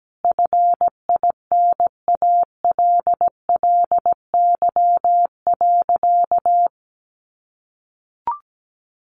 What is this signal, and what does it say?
Morse code, heard as a keyed tone, about 17 wpm, 700 Hz. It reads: FINALLY.
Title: Sentences from Top 1000 Words 17wpm